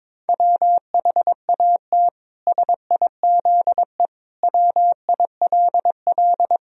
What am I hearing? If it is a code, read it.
Morse code, heard as a keyed tone, about 22 wpm, 700 Hz. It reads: WHAT SIZE WILL
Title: Sentences from Top 1000 Words - Review Part 2 - 22wpm